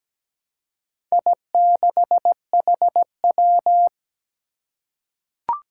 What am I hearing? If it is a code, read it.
Morse code, heard as a keyed tone, about 17 wpm, 700 Hz. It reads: I6HW
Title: Four Challenging Letter-Number Combinations 17wpm